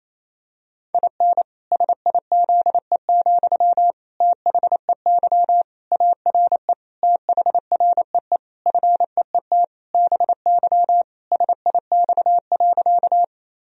Morse code, heard as a keyed tone, about 28 wpm, 700 Hz. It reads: IN SIZE, THEY ARE THREE FEET BY SIX.
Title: Sentences from Top 1000 Words - Review Part 2 - 28wpm